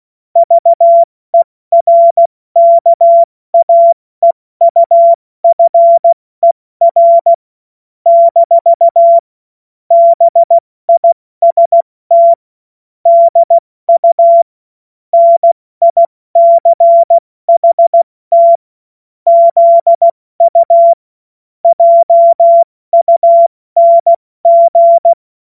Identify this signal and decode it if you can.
Morse code, heard as a keyed tone, about 16 wpm, 680 Hz. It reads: VERKAEUFER - BIST DU NICHT ZU JUNG
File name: Morsekurs / Witze / Witz 03. August 2021, 70 F80